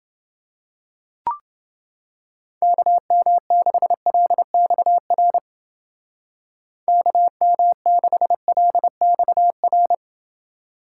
Morse code, heard as a keyed tone, about 28 wpm, 700 Hz. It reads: KM6LXR KM6LXR